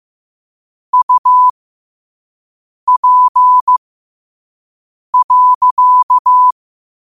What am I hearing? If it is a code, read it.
Morse code, heard as a keyed tone, about 15 wpm, spaced as for 5 wpm, 1000 Hz. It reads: UP.
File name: qotdfiles / 5wpm